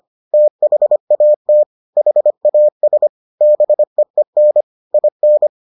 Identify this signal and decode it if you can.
Morse code, heard as a keyed tone, about 25 wpm, 600 Hz. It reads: THAT HAS BEEN IN